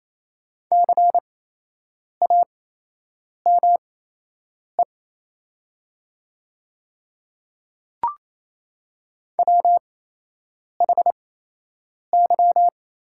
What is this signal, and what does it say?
Morse code, heard as a keyed tone, about 28 wpm, spaced as for 8 wpm, 700 Hz. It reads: CAME WHY
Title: 8x Character Spacing - Top 300 Words - 28wpm